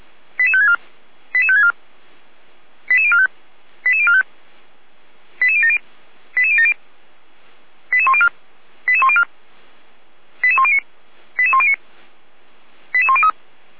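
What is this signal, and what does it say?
Five-tone selective call (ZVEI), heard as quick runs of five tones, tones of about 70 ms. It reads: 88564 88564 88045 88045 88070 88070 88104 88104 88109 88109 88103
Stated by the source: ZVEI five-tone call 88402 received by SDR, on Feuerwehr Orlamünde audio server.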